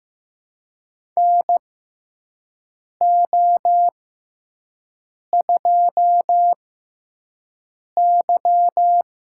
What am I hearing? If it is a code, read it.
Morse code, heard as a keyed tone, about 15 wpm, spaced as for 5 wpm, 700 Hz. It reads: NO2Y
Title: USA Call Signs - 6x Character Spacing - 15wpm